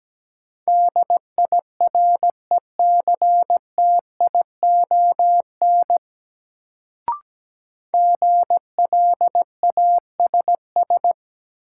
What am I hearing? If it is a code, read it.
Morse code, heard as a keyed tone, about 17 wpm, 700 Hz. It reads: DIRECTION GLASS
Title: Sentences from Top 1300 Words - New Word Drill Alt - 17wpm